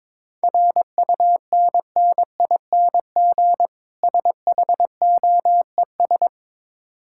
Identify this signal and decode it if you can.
Morse code, heard as a keyed tone, about 22 wpm, 700 Hz. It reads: RUNNING SHOES
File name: Difficult Words with Hints - 22wpm